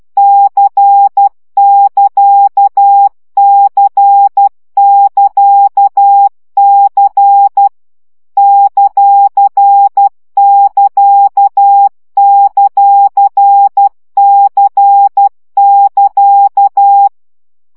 Morse code, heard as a keyed tone, about 12 wpm, 800 Hz. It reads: C<KA>C<KA>C ;<KA>;C<KA>
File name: CW gyakorlas c ct pontosvesszo 10 baud